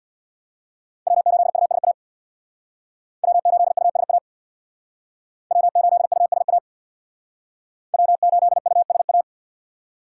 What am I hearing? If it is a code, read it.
Morse code, heard as a keyed tone, about 62 wpm, 700 Hz. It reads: W8USA W8USA W8USA W8USA
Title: Speed Racing - USA 1x3 Call Signs 50wpm